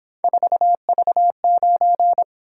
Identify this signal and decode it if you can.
Morse code, heard as a keyed tone, about 26 wpm, 700 Hz. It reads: 4V9
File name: Speed-Racing - Three Letter-Number Combinations - 22wpm